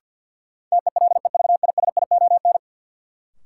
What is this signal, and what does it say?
Morse code, heard as a keyed tone, about 47 wpm, 700 Hz. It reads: TELEVISION
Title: Sentences from Top 1000 Words - Review Part 4 - New Word Drill - 50wpm